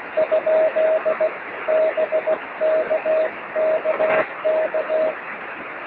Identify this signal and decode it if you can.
Morse code, heard as a keyed tone, about 16 wpm, 660 Hz, with noise. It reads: ? BKKK
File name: sm7lcb / 27-es0u 23cm